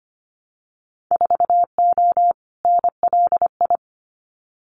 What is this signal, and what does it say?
Morse code, heard as a keyed tone, about 25 wpm, 700 Hz. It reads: 4O NLI
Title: Winter Field Day Exchanges Encore 25wpm